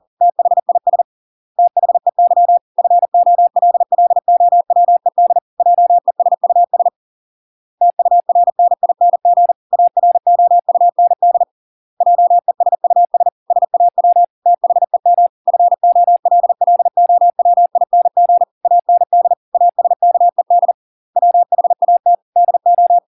Morse code, heard as a keyed tone, about 40 wpm, 700 Hz. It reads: THIS THEY FOLLOWED JESUS TURNING AROUND JESUS SAW THEM FOLLOWING AND ASKED WHAT DO